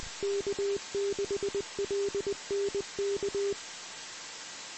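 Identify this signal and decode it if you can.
Morse code, heard as a keyed tone, about 20 wpm, 400 Hz, with noise. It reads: K6LNK
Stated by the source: Morse code received on 2 m, 146.115 MHz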